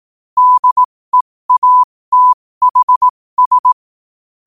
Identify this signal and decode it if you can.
Morse code, heard as a keyed tone, about 18 wpm, spaced as for 15 wpm, 1000 Hz. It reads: DEATHS
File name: qotdfiles / 15wpm